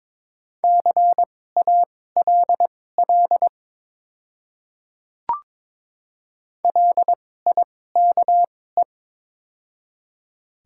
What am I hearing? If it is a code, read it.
Morse code, heard as a keyed tone, about 22 wpm, spaced as for 16 wpm, 690 Hz. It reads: CALL LIKE
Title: 2x Character Spacing - Top 100 Words - 22wpm